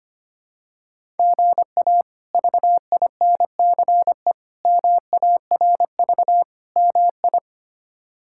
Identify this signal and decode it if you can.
Morse code, heard as a keyed tone, about 25 wpm, 700 Hz. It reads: GA VINCE MARV MI
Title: K1USN SST QSO Practice - 25wpm